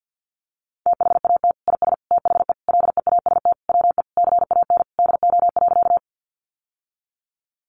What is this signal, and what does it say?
Morse code, heard as a keyed tone, about 50 wpm, 700 Hz. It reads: THAT IS THE LEAST WE CAN DO.
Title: Sentences from Top 700 Words 50wpm